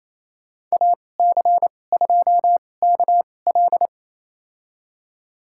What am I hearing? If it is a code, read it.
Morse code, heard as a keyed tone, about 28 wpm, spaced as for 20 wpm, 700 Hz. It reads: AC2KL
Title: USA Call Signs - 2x Character Spacing - 28wpm